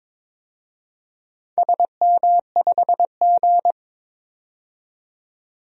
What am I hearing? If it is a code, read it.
Morse code, heard as a keyed tone, about 22 wpm, 700 Hz. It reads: SM5G
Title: DXCC-100 Call Signs - 4 Characters - 22wpm